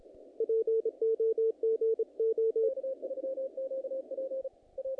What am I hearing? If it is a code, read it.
Morse code, heard as a keyed tone, about 25 wpm, 450 Hz. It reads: POGO
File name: contests / naqp-cw-k5tr-op-k5pi- -2000